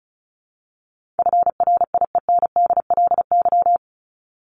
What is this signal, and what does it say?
Morse code, heard as a keyed tone, about 34 wpm, 700 Hz. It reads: FRIENDLY
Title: LY Suffix - Top 100 Words with Base Words 35wpm